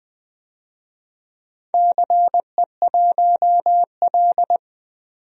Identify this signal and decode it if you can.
Morse code, heard as a keyed tone, about 20 wpm, 700 Hz. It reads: CE1L
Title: DXCC-100 Call Signs - 4 Characters - 20wpm